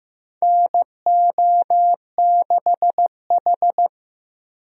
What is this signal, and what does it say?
Morse code, heard as a keyed tone, about 15 wpm, 700 Hz. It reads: NO6H
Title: USA Call Signs - 4 Characters - Repeated 2x - 15wpm